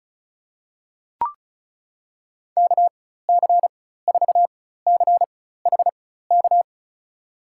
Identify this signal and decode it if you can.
Morse code, heard as a keyed tone, about 34 wpm, spaced as for 16 wpm, 700 Hz. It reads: KC4CHK
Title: USA Call Signs - 6 Characters - 4x Character Spacing - 35wpm